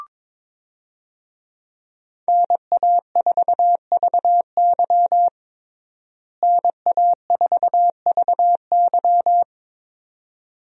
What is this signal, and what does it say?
Morse code, heard as a keyed tone, about 22 wpm, 700 Hz. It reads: NA4VY NA4VY
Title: USA Call Signs - 5 Characters - Repeated 2x - 22wpm